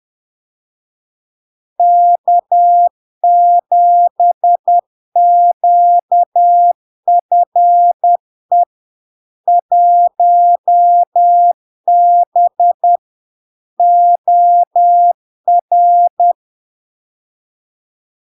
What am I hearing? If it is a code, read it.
Morse code, heard as a keyed tone, about 10 wpm, 700 Hz. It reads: K7QFE 1B OR
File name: LARK-FD-2022-Log-in-CW-10wpm-700-Hz